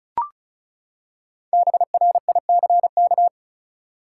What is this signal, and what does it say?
Morse code, heard as a keyed tone, about 34 wpm, 700 Hz. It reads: BRICK
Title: Five Letter Words - 35wpm